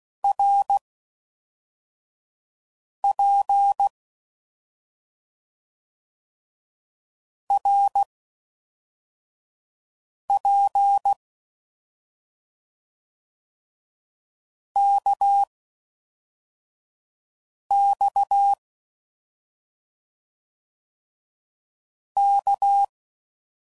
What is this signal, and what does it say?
Morse code, heard as a keyed tone, about 16 wpm, 800 Hz. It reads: R P R P K X K